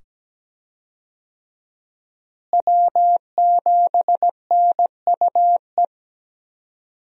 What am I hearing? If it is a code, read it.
Morse code, heard as a keyed tone, about 17 wpm, 700 Hz. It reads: W7NUE
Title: USA Call Signs - 5 Characters - Repeated 2x - 17wpm